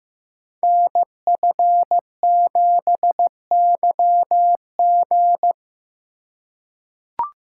Morse code, heard as a keyed tone, about 15 wpm, 700 Hz. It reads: NF7YG
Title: Code Groups - 5 Letter:Numbers - 15wpm